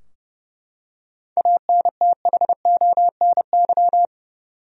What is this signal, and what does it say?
Morse code, heard as a keyed tone, about 30 wpm, 700 Hz. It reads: ANTHONY
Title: USA Common Names - 30wpm